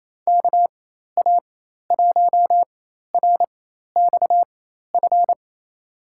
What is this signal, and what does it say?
Morse code, heard as a keyed tone, about 28 wpm, spaced as for 13 wpm, 700 Hz. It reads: KA1RXF
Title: USA Call Signs - 4x Character Spacing - 28wpm